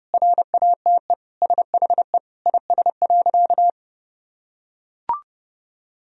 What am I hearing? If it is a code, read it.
Morse code, heard as a keyed tone, about 32 wpm, 700 Hz. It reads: RATE SHE IS.
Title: Sentences from Top 1000 Words - Review Part 4 - 30wpm